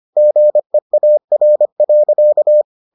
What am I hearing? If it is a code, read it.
Morse code, heard as a keyed tone, about 25 wpm, 600 Hz. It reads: GEAR.